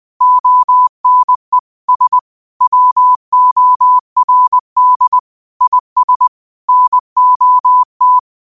Morse code, heard as a keyed tone, about 20 wpm, 1000 Hz. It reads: ONE S WORD IS NOT